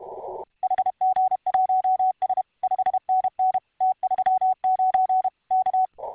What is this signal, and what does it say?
Morse code, heard as a keyed tone, about 32 wpm, 750 Hz. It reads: HG1S 5NN T39 K